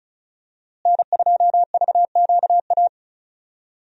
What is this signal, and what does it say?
Morse code, heard as a keyed tone, about 36 wpm, 700 Hz. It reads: N2VQA
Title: USA Call Signs - 5 Characters - Repeated 2x - 35wpm